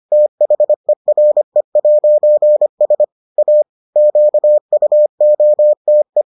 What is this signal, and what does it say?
Morse code, heard as a keyed tone, about 25 wpm, 600 Hz. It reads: THERE'S A QUOTE